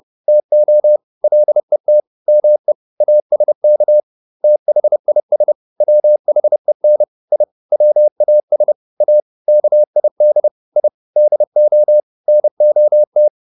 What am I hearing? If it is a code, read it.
Morse code, heard as a keyed tone, about 30 wpm, 600 Hz. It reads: TO LET ME ASK THIS WHEN I WAS A KID I DO NOT